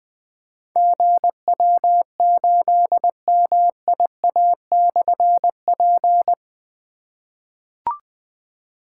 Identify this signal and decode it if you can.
Morse code, heard as a keyed tone, about 20 wpm, 700 Hz. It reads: GW8MIA/P
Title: Difficult Call Signs 20wpm